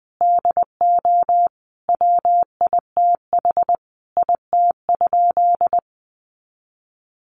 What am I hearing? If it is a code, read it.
Morse code, heard as a keyed tone, about 20 wpm, 700 Hz. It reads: DO WITH IT?
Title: Sentences from Top 100 Words 20wpm